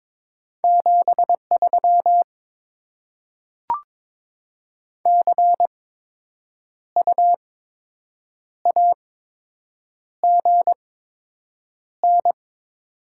Morse code, heard as a keyed tone, about 22 wpm, spaced as for 7 wpm, 700 Hz. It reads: 73 CU AGN
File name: Top 20 QSO Elements - 8x Character Spacing - 22wpm